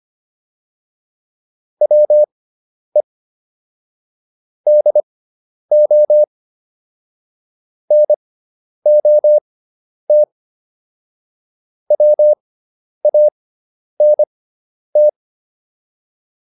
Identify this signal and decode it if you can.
Morse code, heard as a keyed tone, about 25 wpm, spaced as for 10 wpm, 600 Hz. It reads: WE DO NOT WANT